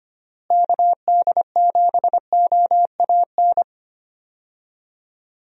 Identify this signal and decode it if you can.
Morse code, heard as a keyed tone, about 25 wpm, 700 Hz. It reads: KD7OAN
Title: USA Call Signs - 6 Characters - Repeated 3x - 25wpm